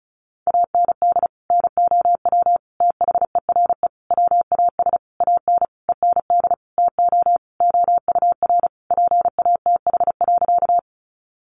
Morse code, heard as a keyed tone, about 34 wpm, 700 Hz. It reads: AND NOW THERE WAS AN END TO OUR PATH.
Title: Sentences from Top 1400 Words 35wpm